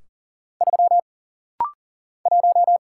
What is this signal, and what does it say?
Morse code, heard as a keyed tone, about 40 wpm, 700 Hz. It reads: 3 1
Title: Single Letter-Number Alt - Rapid-Fire 40wpm